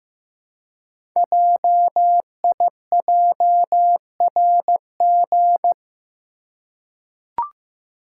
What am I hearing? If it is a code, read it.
Morse code, heard as a keyed tone, about 15 wpm, 700 Hz. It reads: JIJRG
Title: Code Groups - 5 Letters - 15wpm